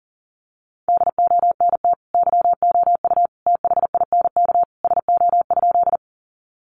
Morse code, heard as a keyed tone, about 40 wpm, 700 Hz. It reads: DONT YOU THINK SO?
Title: Sentences from Top 300 Words 40wpm